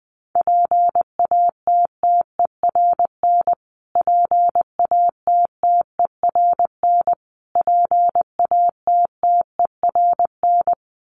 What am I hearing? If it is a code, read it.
Morse code, heard as a keyed tone, about 20 wpm, 700 Hz. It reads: PATTERN PATTERN PATTERN
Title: Top 500 Words Repeated 20wpm